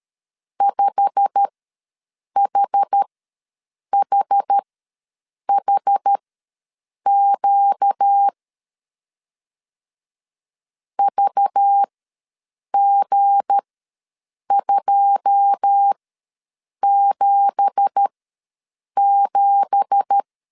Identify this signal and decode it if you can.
Morse code, heard as a keyed tone, about 13 wpm, spaced as for 7 wpm, 800 Hz. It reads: 5HHHQ VG277
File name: LEZIONI / 9 2867 ws 12wpm 3431